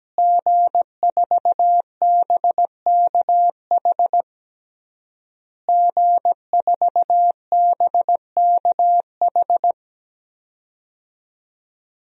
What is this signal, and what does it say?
Morse code, heard as a keyed tone, about 17 wpm, 700 Hz. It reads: G4BKH G4BKH
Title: DXCC-100 Call Signs - 5 Characters - Repeated 3x - 17wpm